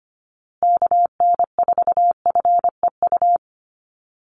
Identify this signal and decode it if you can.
Morse code, heard as a keyed tone, about 25 wpm, 700 Hz. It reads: KN4FEU
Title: USA Call Signs Repeated 2x 25wpm